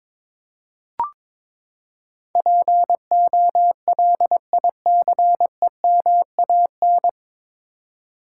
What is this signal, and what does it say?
Morse code, heard as a keyed tone, about 22 wpm, 700 Hz. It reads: POLICEMAN
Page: Morse Code Ninja practice 22wpm